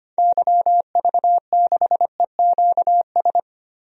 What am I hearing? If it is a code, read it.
Morse code, heard as a keyed tone, about 25 wpm, 700 Hz. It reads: YV6EQS
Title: DXCC-100 Call Signs - 6 Characters - Repeated 3x - 25wpm